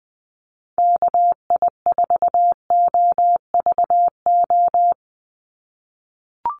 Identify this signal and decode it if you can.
Morse code, heard as a keyed tone, about 20 wpm, 700 Hz. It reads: KI4OVO
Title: USA Call Signs 20wpm